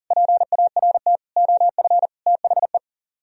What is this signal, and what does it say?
Morse code, heard as a keyed tone, about 51 wpm, 700 Hz. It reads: PART OF THE